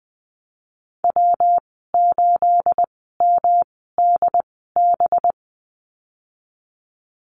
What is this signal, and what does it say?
Morse code, heard as a keyed tone, about 20 wpm, spaced as for 15 wpm, 700 Hz. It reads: W8MDB
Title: 238 Lesson 38g - Letter q - Call Signs 20wpm